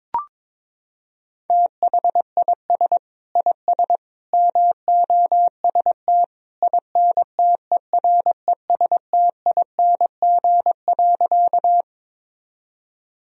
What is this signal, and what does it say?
Morse code, heard as a keyed tone, about 21 wpm, 700 Hz. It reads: THIS IS MOST INTERESTING.
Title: Sentences from Top 1000 Words - Review Part 1 - 22wpm